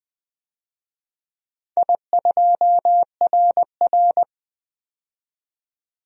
Morse code, heard as a keyed tone, about 20 wpm, 700 Hz. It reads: I2RR
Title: DXCC-100 Call Signs - 4 Characters - 20wpm